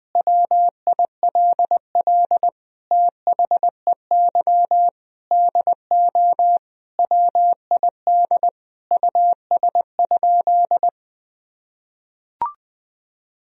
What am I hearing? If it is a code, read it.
Morse code, heard as a keyed tone, about 21 wpm, 700 Hz. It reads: WILL THEY DO WID US?
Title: Sentences from Top 200 Words and Abbreviations 20wpm